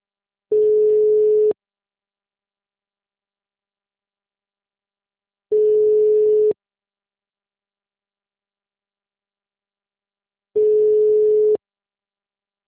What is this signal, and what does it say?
Ringback tone (Europe, 425 Hz).
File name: out- -2007- - - .35871